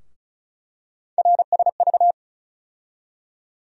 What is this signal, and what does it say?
Morse code, heard as a keyed tone, about 36 wpm, 700 Hz. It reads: RSV
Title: USA Call Sign Postfixes - 3 Letters 35wpm